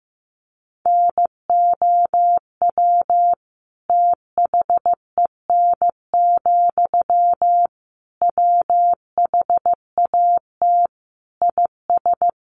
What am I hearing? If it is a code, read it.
Morse code, heard as a keyed tone, about 15 wpm, 700 Hz. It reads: NOW THEN, WHAT IS
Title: Sentences from Top 1100 Words 15wpm